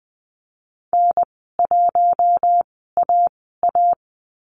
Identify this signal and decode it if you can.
Morse code, heard as a keyed tone, about 20 wpm, spaced as for 15 wpm, 700 Hz. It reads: N1AA
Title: 186 Lesson 30g - Letter b - Call Signs 20wpm